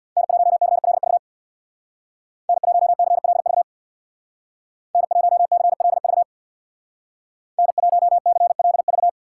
Morse code, heard as a keyed tone, about 62 wpm, 700 Hz. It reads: N1CLV N1CLV N1CLV N1CLV
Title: Speed Racing - USA 1x3 Call Signs 50wpm